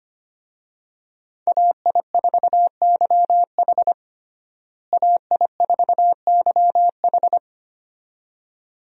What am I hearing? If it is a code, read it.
Morse code, heard as a keyed tone, about 25 wpm, 700 Hz. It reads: AI4YH AI4YH
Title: USA Call Signs - 5 Characters - Repeated 2x - 25wpm